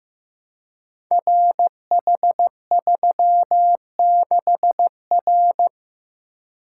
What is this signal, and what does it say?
Morse code, heard as a keyed tone, about 15 wpm, 700 Hz. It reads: RH36R
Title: Code Groups - 5 Letter:Numbers - 15wpm